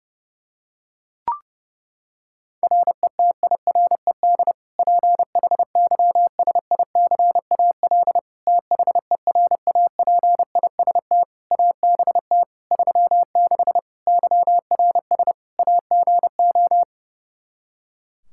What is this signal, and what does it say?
Morse code, heard as a keyed tone, about 32 wpm, 700 Hz. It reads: RETIRED PHYSICAL THERAPIST ABT 36 YRS AGO
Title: Practice QSO - Exchange 3 - More Info - 30wpm